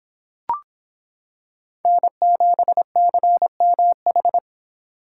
Morse code, heard as a keyed tone, about 25 wpm, 700 Hz. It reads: N7CMH